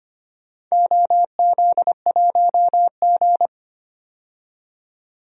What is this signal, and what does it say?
Morse code, heard as a keyed tone, about 25 wpm, 700 Hz. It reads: OZ1G
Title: DXCC-100 Call Signs - 4 Characters - 25wpm